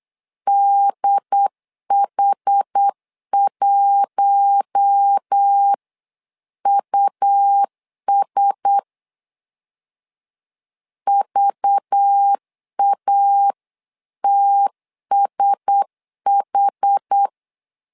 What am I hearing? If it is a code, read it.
Morse code, heard as a keyed tone, about 8 wpm, 800 Hz. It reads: DH1US VATSH
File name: LEZIONI / 5 LBVC / 5 LBV 8wpm 2837